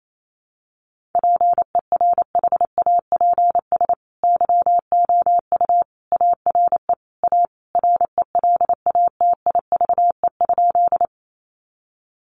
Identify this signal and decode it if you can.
Morse code, heard as a keyed tone, about 28 wpm, 700 Hz. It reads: PERHAPS YOU ARE A RELATIVE?
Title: Sentences from Top 1400 Words 28wpm